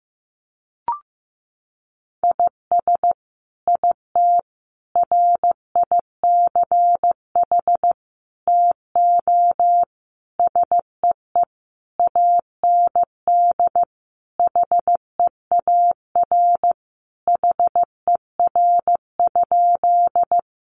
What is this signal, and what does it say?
Morse code, heard as a keyed tone, about 15 wpm, 700 Hz. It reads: IS IT RICH TO SEE AND HEAR HER?
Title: Sentences from Top 1100 Words 15wpm